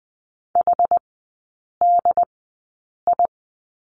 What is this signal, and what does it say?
Morse code, heard as a keyed tone, about 20 wpm, spaced as for 8 wpm, 700 Hz. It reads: HDI
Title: 067 Lesson 12e - Letter d - Three Characters 20wpm